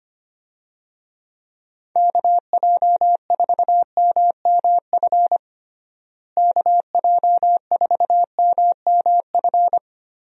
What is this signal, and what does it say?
Morse code, heard as a keyed tone, about 25 wpm, 700 Hz. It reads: KJ4MMF KJ4MMF
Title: USA Call Signs - 6 Characters - Repeated 3x - 25wpm